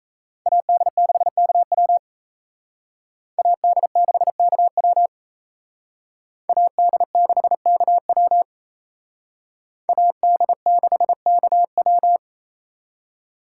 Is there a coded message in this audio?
Yes: Morse code from 0.5 to 12.2 s, heard as a keyed tone, about 35 wpm, 700 Hz. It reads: AD6KW AD6KW AD6KW AD6KW